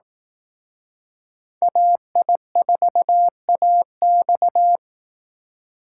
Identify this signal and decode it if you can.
Morse code, heard as a keyed tone, about 18 wpm, 700 Hz. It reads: AI4AX
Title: Speed Racing - USA 2x2 Call Signs 15wpm